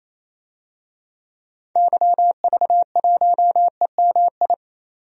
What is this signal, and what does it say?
Morse code, heard as a keyed tone, about 27 wpm, 700 Hz. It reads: YV1EMI
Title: DXCC-100 Call Signs - 6 Characters - 28wpm